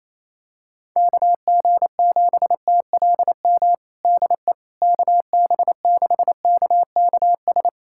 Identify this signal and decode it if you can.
Morse code, heard as a keyed tone, about 27 wpm, 700 Hz. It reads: KG7TLM DE KB6KKS
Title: USA Callsign DE Callsign - 6 Characters - 28wpm